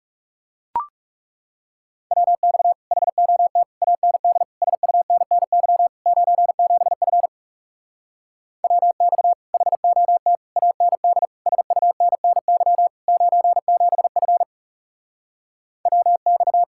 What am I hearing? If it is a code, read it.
Morse code, heard as a keyed tone, about 40 wpm, 700 Hz. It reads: WX HOT AND SUNNY 97F WX HOT AND SUNNY 97F WX